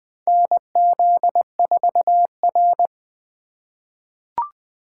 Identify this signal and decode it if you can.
Morse code, heard as a keyed tone, about 20 wpm, 700 Hz. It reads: NZ4R
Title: 2023 SOTA Activator Call Signs - 20wpm